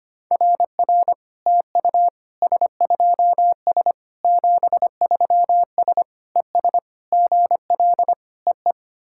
Morse code, heard as a keyed tone, about 25 wpm, 690 Hz. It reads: RR TU S2S 73S ES GL EE
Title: SOTA QSO Practice 25wpm